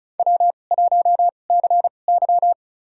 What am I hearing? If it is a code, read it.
Morse code, heard as a keyed tone, about 36 wpm, spaced as for 25 wpm, 690 Hz. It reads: W1CY
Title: USA Call Signs - 4 Characters - 2x Character Spacing - 35wpm